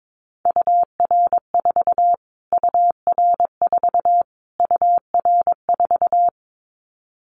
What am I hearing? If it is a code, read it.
Morse code, heard as a keyed tone, about 22 wpm, 700 Hz. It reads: UR4 UR4 UR4
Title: Callsign Prefixes for Top 25 DXCC - Repeated 3x - 22wpm